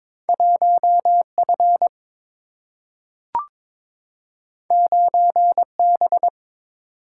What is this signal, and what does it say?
Morse code, heard as a keyed tone, about 22 wpm, 700 Hz. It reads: 1F 9B
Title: Single-Letter Single-Number 22wpm